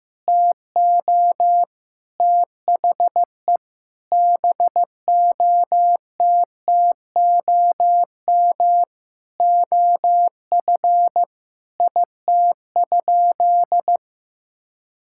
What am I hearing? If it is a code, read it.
Morse code, heard as a keyed tone, about 16 wpm, 700 Hz. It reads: TO THE BOTTOM OF IT?
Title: Sentences from Top 1000 Words - Review Part 6 - 15wpm